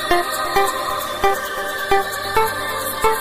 There is no code.